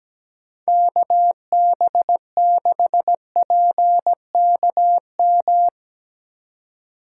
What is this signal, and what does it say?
Morse code, heard as a keyed tone, about 17 wpm, 700 Hz. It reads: KB6PKM
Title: USA Call Signs - 2x Character Spacing - 17wpm